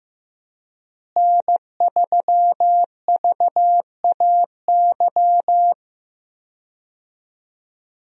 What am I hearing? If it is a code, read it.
Morse code, heard as a keyed tone, about 15 wpm, 700 Hz. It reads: N3VAY
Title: USA Call Signs - 5 Characters - Repeated 2x - 15wpm